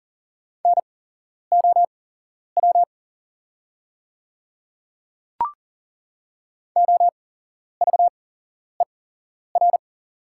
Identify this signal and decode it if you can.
Morse code, heard as a keyed tone, about 40 wpm, spaced as for 11 wpm, 700 Hz. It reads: NOW OVER
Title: 8x Character Spacing - Top 200 Words - 40wpm